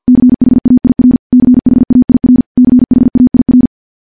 Morse code, heard as a keyed tone, about 50 wpm, 250 Hz. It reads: CHAIR CHAIR CHAIR
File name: morse / morse-training-50WPM